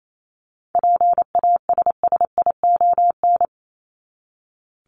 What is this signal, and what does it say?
Morse code, heard as a keyed tone, about 28 wpm, 700 Hz. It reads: PASSION
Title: Top 1001-1250 Words 28wpm